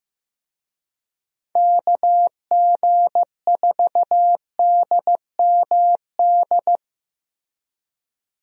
Morse code, heard as a keyed tone, about 15 wpm, 700 Hz. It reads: KG4DMD
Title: USA Call Signs - 6 Characters - Repeated 3x - 15wpm